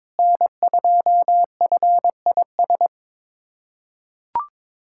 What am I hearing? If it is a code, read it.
Morse code, heard as a keyed tone, about 22 wpm, 700 Hz. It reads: N2FIS
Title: USA Callsign DE Callsign - 5 Characters - 22wpm